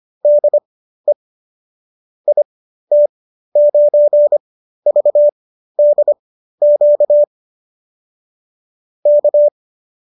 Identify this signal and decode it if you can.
Morse code, heard as a keyed tone, about 25 wpm, spaced as for 13 wpm, 590 Hz. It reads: DE IT9VDQ K